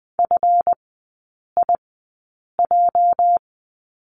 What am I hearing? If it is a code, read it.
Morse code, heard as a keyed tone, about 20 wpm, spaced as for 8 wpm, 700 Hz. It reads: FIJ
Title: 205 Lesson 33e - Letter j - Three Characters 20wpm